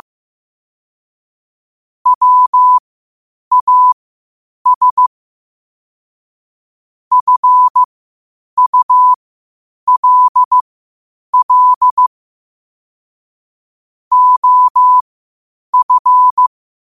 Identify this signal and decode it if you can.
Morse code, heard as a keyed tone, about 15 wpm, spaced as for 8 wpm, 1000 Hz. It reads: WAS FULL OF
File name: qotdfiles / 8wpm